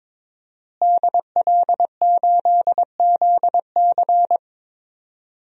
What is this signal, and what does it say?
Morse code, heard as a keyed tone, about 22 wpm, 700 Hz. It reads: DL8ZC
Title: DXCC-100 Call Signs - 5 Characters - Repeated 3x - 22wpm